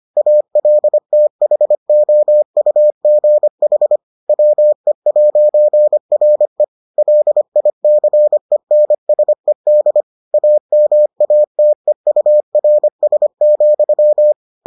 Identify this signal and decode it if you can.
Morse code, heard as a keyed tone, about 25 wpm, 600 Hz. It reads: ALTHOUGH WE'RE LICENSED AMATEURS,